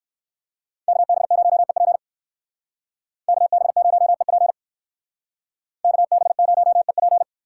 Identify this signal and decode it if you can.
Morse code, heard as a keyed tone, about 55 wpm, 700 Hz. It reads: KB0EP KB0EP KB0EP
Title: Speed Racing - USA 2x2 Call Signs 45wpm